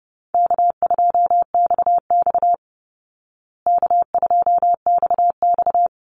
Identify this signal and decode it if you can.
Morse code, heard as a keyed tone, about 30 wpm, 700 Hz. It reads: K2XX K2XX
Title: Call Signs Repeated 3x - 30wpm